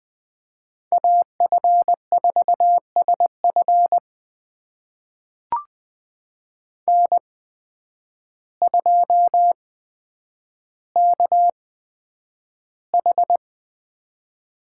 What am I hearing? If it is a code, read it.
Morse code, heard as a keyed tone, about 20 wpm, spaced as for 9 wpm, 700 Hz. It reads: AF4SF N2KH
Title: USA Call Signs - 5 Characters - 8x Character Spacing - 20wpm